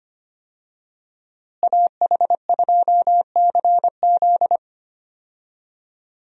Morse code, heard as a keyed tone, about 25 wpm, 700 Hz. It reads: AH2CZ
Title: USA Call Signs - 5 Characters - Repeated 3x - 25wpm